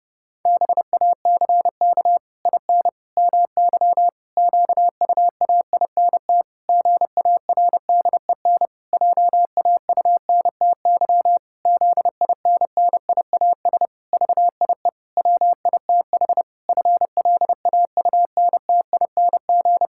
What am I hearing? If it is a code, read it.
Morse code, heard as a keyed tone, about 30 wpm, 700 Hz. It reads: BACK IN MY QUAINT GARDEN JAUNTY ZINNIAS VIE WITH FLAUNTING
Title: Pangram Challenge v2 - 30wpm